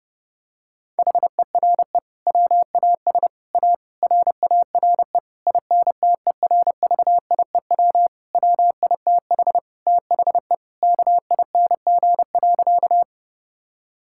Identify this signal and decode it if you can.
Morse code, heard as a keyed tone, about 30 wpm, 700 Hz. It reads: HERE WAS A RARE INTERVIEW WITH THE KING.
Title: Sentences from Top 1600 Words 30wpm